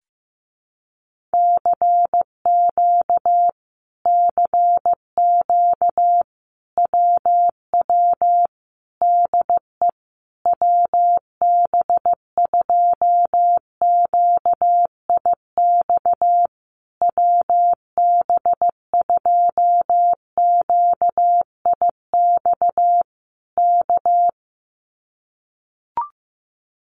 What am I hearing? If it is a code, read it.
Morse code, heard as a keyed tone, about 15 wpm, 700 Hz. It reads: CQ CQ WW DE WB2QIX WB2QIX K
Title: Common QSO Phrases Encore 15wpm